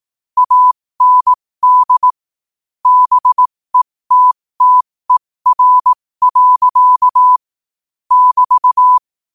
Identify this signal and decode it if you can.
Morse code, heard as a keyed tone, about 18 wpm, spaced as for 15 wpm, 1000 Hz. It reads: AND BETTER. <BT>
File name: qotdfiles / 15wpm